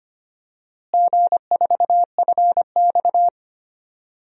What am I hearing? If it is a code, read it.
Morse code, heard as a keyed tone, about 25 wpm, 700 Hz. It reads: G4FX
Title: DXCC-100 Call Signs - 4 Characters - Repeated 2x - 25wpm